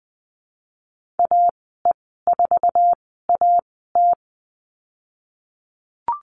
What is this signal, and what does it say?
Morse code, heard as a keyed tone, about 20 wpm, spaced as for 15 wpm, 700 Hz. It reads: AE4AT
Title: 048 Lesson 9g - Number 4 - Call Signs 20wpm